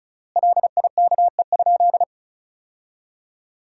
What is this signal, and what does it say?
Morse code, heard as a keyed tone, about 34 wpm, 700 Hz. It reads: LIKE?
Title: 3x Word-Spacing - Sentences from Top 100 Words - 35wpm